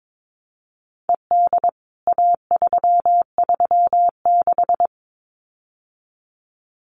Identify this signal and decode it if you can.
Morse code, heard as a keyed tone, about 22 wpm, 700 Hz. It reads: ED A336
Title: CWT Exchange 22wpm